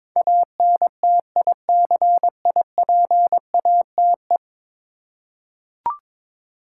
Morse code, heard as a keyed tone, about 22 wpm, 700 Hz. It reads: ANTICIPATE
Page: morse-code-ninja-practice-22wpm